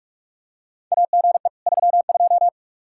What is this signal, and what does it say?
Morse code, heard as a keyed tone, about 42 wpm, 700 Hz. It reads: AGE 32
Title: K1USN SST QSO Practice 45wpm